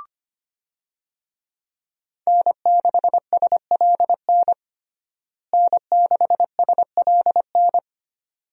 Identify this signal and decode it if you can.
Morse code, heard as a keyed tone, about 25 wpm, 700 Hz. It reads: N6SLN N6SLN
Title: USA Call Signs - 5 Characters - Repeated 2x - 25wpm